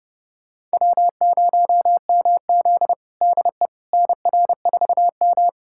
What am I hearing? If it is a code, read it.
Morse code, heard as a keyed tone, about 30 wpm, 700 Hz. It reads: W0MZ DE NR4M
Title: USA Callsign DE Callsign - 4 Characters - 30wpm